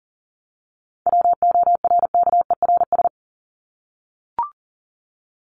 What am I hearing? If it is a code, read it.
Morse code, heard as a keyed tone, about 40 wpm, 700 Hz. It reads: WORKERS
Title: ES:S Suffix - Top 100 Words with Base Words 40wpm